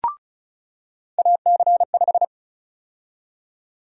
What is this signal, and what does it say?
Morse code, heard as a keyed tone, about 34 wpm, 700 Hz. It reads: AC5
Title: USA Call Sign Prefixes - 35wpm